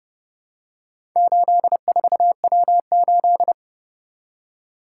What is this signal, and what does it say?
Morse code, heard as a keyed tone, about 30 wpm, 700 Hz. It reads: 84W8